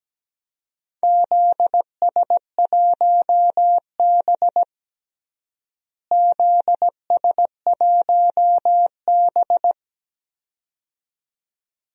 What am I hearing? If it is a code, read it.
Morse code, heard as a keyed tone, about 17 wpm, 700 Hz. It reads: ZS1B ZS1B
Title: DXCC-100 Call Signs - 4 Characters - Repeated 2x - 17wpm